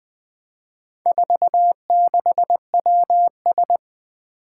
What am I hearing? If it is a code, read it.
Morse code, heard as a keyed tone, about 20 wpm, 700 Hz. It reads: 46WS
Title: Four Challenging Letter-Number Combinations 20wpm